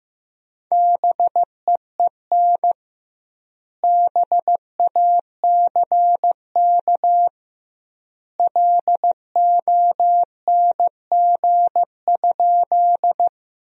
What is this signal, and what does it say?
Morse code, heard as a keyed tone, about 15 wpm, 700 Hz. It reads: BEEN BACK LONG?
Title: 2x Word-Spacing - Sentences from Top 200 Words - 15wpm